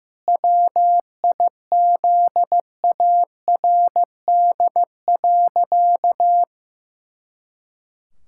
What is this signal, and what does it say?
Morse code, heard as a keyed tone, about 15 wpm, 700 Hz. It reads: WIZARD.
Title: Pangram Challenge v2 - 15wpm